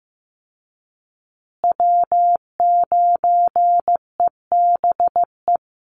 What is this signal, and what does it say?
Morse code, heard as a keyed tone, about 15 wpm, 700 Hz. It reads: W9EBE
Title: Call Signs Repeated 3x - 15wpm